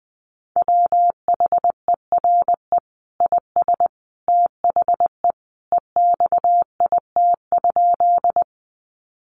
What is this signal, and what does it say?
Morse code, heard as a keyed tone, about 20 wpm, 700 Hz. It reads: WHERE IS THE EXIT?
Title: Sentences from Top 1100 Words 20wpm